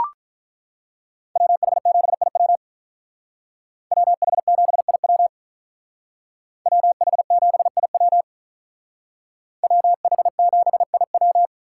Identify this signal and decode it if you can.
Morse code, heard as a keyed tone, about 42 wpm, 700 Hz. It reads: WH7IW WH7IW WH7IW WH7IW
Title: Speed Racing - USA 2x2 Call Signs 35wpm